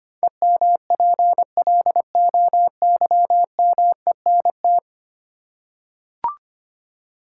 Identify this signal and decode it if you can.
Morse code, heard as a keyed tone, about 25 wpm, 700 Hz. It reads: EMPLOYMENT
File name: 10 Letter Words - First 3 Letters Given - 25wpm